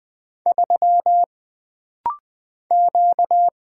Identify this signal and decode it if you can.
Morse code, heard as a keyed tone, about 20 wpm, 700 Hz. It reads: 3 Q
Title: Single Character (Letter:Number:Punctuation:Prosign) - Rapid-Fire - 20wpm